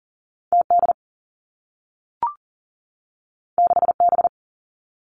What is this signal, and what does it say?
Morse code, heard as a keyed tone, about 39 wpm, 700 Hz. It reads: T D 6 B
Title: Two Challenging Letter-Number Combinations 40wpm